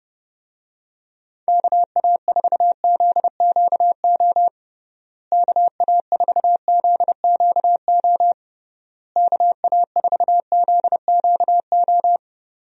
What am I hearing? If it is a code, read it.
Morse code, heard as a keyed tone, about 30 wpm, 700 Hz. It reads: KA4ZQO KA4ZQO KA4ZQO
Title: USA Call Signs - 6 Characters - Repeated 3x - 30wpm